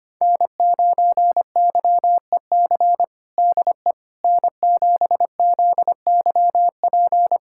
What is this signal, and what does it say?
Morse code, heard as a keyed tone, about 25 wpm, 700 Hz. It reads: N9YEC DE N7ZYP
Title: USA Callsign DE Callsign - 5 Characters - 25wpm